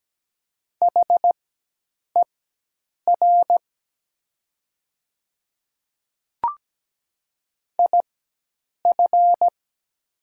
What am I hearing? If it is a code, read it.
Morse code, heard as a keyed tone, about 17 wpm, spaced as for 8 wpm, 700 Hz. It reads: HER IF
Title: 4x Character Spacing - Top 200 Words - 17wpm